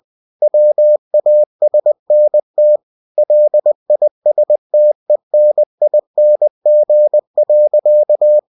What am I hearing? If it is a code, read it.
Morse code, heard as a keyed tone, about 20 wpm, 600 Hz. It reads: WASNT LISTENING.
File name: cw-books / winnie-the-pooh / winnie-20wpm-page-0058